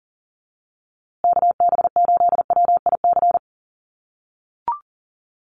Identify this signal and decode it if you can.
Morse code, heard as a keyed tone, about 40 wpm, 700 Hz. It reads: KB8WIC
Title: USA Call Signs Repeated 2x 40wpm